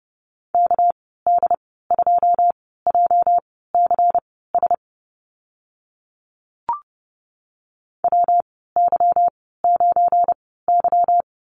Morse code, heard as a keyed tone, about 30 wpm, spaced as for 17 wpm, 700 Hz. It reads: KD2JCS WY9Y
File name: Lesson 37g - Letter x - Call Signs 30wpm